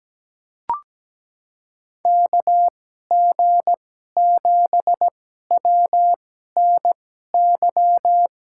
Morse code, heard as a keyed tone, about 17 wpm, spaced as for 12 wpm, 700 Hz. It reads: KG7WNY
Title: USA Call Signs - 6 Characters - 2x Character Spacing - 17wpm